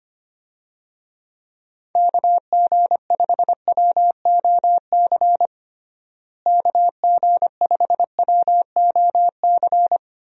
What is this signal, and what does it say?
Morse code, heard as a keyed tone, about 25 wpm, 700 Hz. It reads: KG5WOC KG5WOC
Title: USA Call Signs - 6 Characters - Repeated 3x - 25wpm